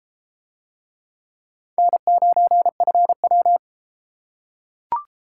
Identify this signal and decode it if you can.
Morse code, heard as a keyed tone, about 33 wpm, 700 Hz. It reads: N9FW